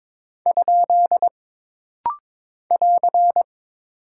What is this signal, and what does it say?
Morse code, heard as a keyed tone, about 23 wpm, 700 Hz. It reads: ? <AR>
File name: practice / 22wpm / Single Punctuation-Prosign - Rapid-Fire - 22wpm